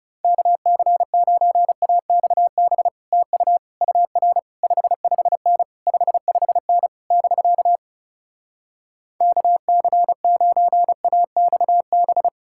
Morse code, heard as a keyed tone, about 38 wpm, 700 Hz. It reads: KC9AXB TU UR 55N 55N <BK> KC9AXB